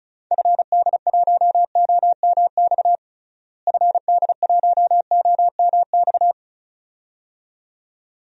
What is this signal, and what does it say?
Morse code, heard as a keyed tone, about 34 wpm, 700 Hz. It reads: FD1OMX FD1OMX
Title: DXCC-100 Call Signs - 6 Characters - Repeated 2x - 35wpm